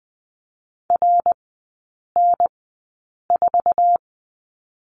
Morse code, heard as a keyed tone, about 20 wpm, spaced as for 8 wpm, 700 Hz. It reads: RN4
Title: 060 Lesson 11e - Letter h - Three Characters 20wpm